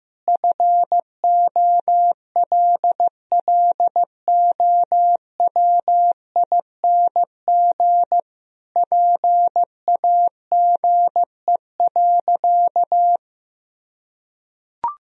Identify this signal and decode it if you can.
Morse code, heard as a keyed tone, about 15 wpm, 700 Hz. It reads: FOLLOWING PAGE.
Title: Sentences from Top 1000 Words - Review Part 1 - 15wpm